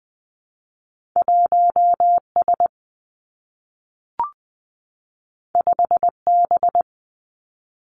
Morse code, heard as a keyed tone, about 20 wpm, 700 Hz. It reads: 1 S 5 B
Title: Two Challenging Letter-Number Combinations 20wpm